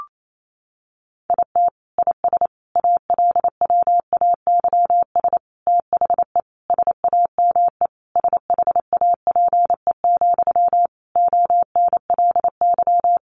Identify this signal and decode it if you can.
Morse code, heard as a keyed tone, about 28 wpm, 700 Hz. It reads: IT IS ALWAYS THE SAME SHAPE, ONLY
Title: Sentences from Top 1400 Words 28wpm